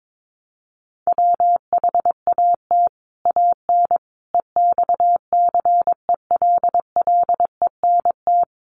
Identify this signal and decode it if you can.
Morse code, heard as a keyed tone, about 22 wpm, 700 Hz. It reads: WHAT AN EXCELLENT
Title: Sentences from Top 1000 Words 22wpm